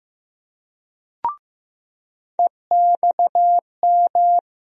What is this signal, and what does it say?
Morse code, heard as a keyed tone, about 15 wpm, 700 Hz. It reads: EXM
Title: USA Call Sign Postfixes - 3 Letters 15wpm